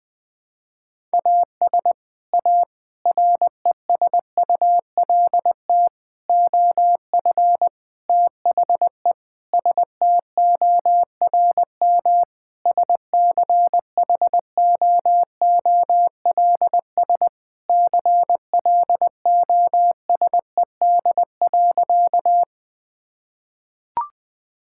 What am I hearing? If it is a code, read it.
Morse code, heard as a keyed tone, about 21 wpm, 700 Hz. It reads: AS A RESULT OF THE STORM SCHOOLS CLOSED.
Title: Sentences with 5-Word Phrases - Spoken Beginning - 20wpm